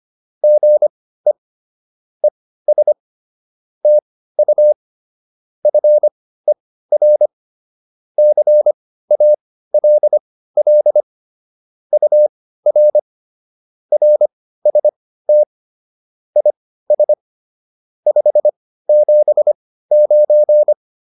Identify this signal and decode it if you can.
Morse code, heard as a keyed tone, about 25 wpm, spaced as for 15 wpm, 600 Hz. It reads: GE ES TU FER CALL UR RST IS 579